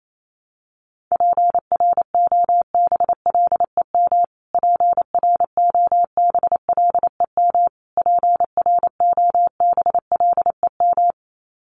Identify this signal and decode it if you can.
Morse code, heard as a keyed tone, about 28 wpm, 700 Hz. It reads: PROBLEM PROBLEM PROBLEM
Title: Top 500 Words Repeated 28wpm